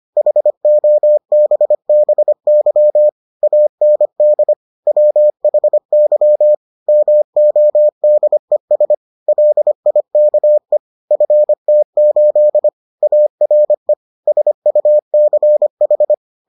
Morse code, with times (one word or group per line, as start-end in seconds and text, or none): 0.2-3.1 HOBBY
3.4-4.5 AND
4.9-6.5 WHY
6.9-8.9 MODES
9.3-10.8 LIKE
11.1-12.7 FT8
13.0-13.9 ARE
14.3-16.1 SUCH